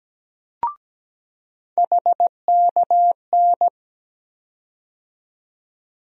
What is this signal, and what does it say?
Morse code, heard as a keyed tone, about 17 wpm, 700 Hz. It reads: HKN
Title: USA Call Sign Postfixes - 3 Letters 17wpm